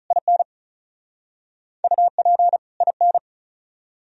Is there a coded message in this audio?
Yes: Morse code from 0.1 to 3.2 s, heard as a keyed tone, about 47 wpm, 700 Hz. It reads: IN UP IN